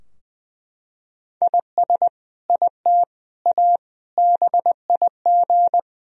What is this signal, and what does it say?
Morse code, heard as a keyed tone, about 20 wpm, 700 Hz. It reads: IS IT A BIG
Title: Sentences from Top 1000 Words - Review Part 3 - 20wpm